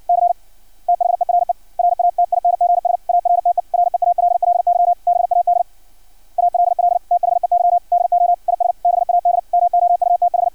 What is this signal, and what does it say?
Morse code, heard as a keyed tone, about 56 wpm, 700 Hz. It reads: C THERE RATINGS ARTE REALLY BAD ALL THEY DO IS BAD MOUTH
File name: archived / CRSnet